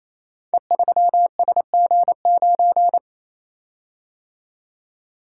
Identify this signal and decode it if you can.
Morse code, heard as a keyed tone, about 27 wpm, 700 Hz. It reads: E3SG9
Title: Code Groups - 5 Letter:Numbers - 28wpm